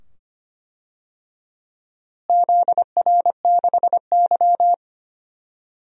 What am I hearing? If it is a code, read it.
Morse code, heard as a keyed tone, about 25 wpm, 700 Hz. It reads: ZR6Y
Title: DXCC-100 Call Signs - 4 Characters - Repeated 2x - 25wpm